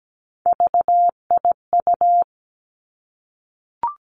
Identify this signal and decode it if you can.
Morse code, heard as a keyed tone, about 17 wpm, 700 Hz. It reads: VIU
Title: Three Challenging Letter-Number Combinations 17wpm